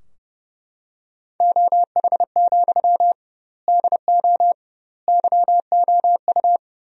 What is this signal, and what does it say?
Morse code, heard as a keyed tone, about 30 wpm, 700 Hz. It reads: OH, DO YOU
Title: 2x Word-Spacing - Sentences from Top 500 Words - 30wpm